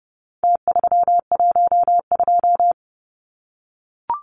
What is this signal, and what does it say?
Morse code, heard as a keyed tone, about 30 wpm, 700 Hz. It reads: T312
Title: Numbers - Contest Style - 4-Digit 30wpm